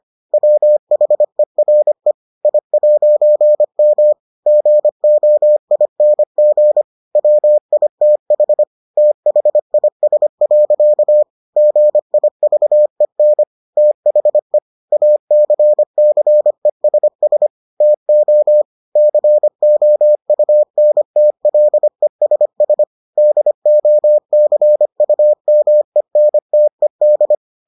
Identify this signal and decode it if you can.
Morse code, heard as a keyed tone, about 25 wpm, 600 Hz. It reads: WHERE I'M GOING WITH THIS. GIVEN THE ACCESS TO COUNTLESS DOCUMENTED